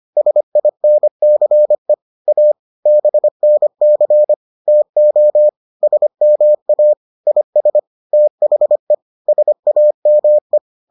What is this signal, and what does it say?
Morse code, heard as a keyed tone, about 25 wpm, 600 Hz. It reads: SINCE A BNC TO SMA IS THE SAME